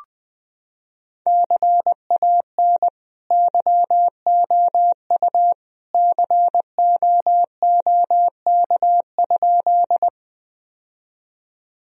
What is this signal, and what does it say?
Morse code, heard as a keyed tone, about 20 wpm, 700 Hz. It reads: CAN YOU COOK?
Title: Sentences from Top 1000 Words - Review Part 4 - 20wpm